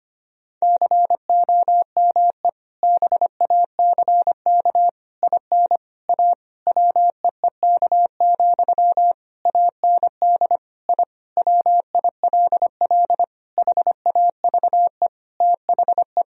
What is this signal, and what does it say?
Morse code, heard as a keyed tone, about 25 wpm, 700 Hz. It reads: COME BACK IN A WEEK, AND I WILL HAVE THE
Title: Sentences from Top 1000 Words - Review Part 3 - 25wpm